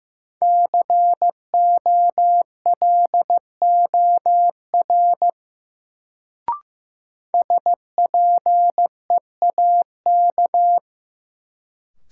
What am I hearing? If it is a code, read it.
Morse code, heard as a keyed tone, about 15 wpm, 700 Hz. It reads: COLOR SPEAK
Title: Sentences from Top 1000 Words - Review Part 2 - New Word Drill - 15wpm